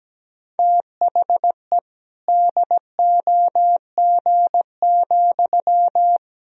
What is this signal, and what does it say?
Morse code, heard as a keyed tone, about 17 wpm, 700 Hz. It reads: THE DOG,
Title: Sentences from Top 1000 Words - Review Part 3 - 17wpm